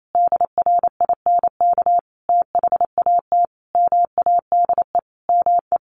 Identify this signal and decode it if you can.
Morse code, heard as a keyed tone, about 28 wpm, 700 Hz. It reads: DRINK THAT MADE ME
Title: Sentences from Top 1500 Words 28wpm